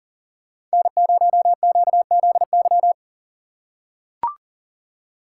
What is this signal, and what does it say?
Morse code, heard as a keyed tone, about 40 wpm, 700 Hz. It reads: N0QZY
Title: USA Call Signs - 5 Characters - 2x Character Spacing - 40wpm